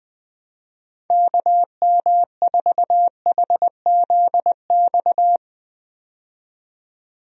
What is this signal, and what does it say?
Morse code, heard as a keyed tone, about 21 wpm, 700 Hz. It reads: KM4HZX
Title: USA Call Signs - 6 Characters - 20wpm